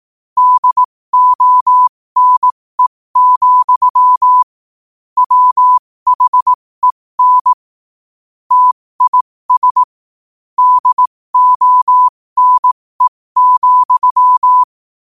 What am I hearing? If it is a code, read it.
Morse code, heard as a keyed tone, about 18 wpm, spaced as for 15 wpm, 1000 Hz. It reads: DONE, WHEN TIS DONE,
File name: qotdfiles / 15wpm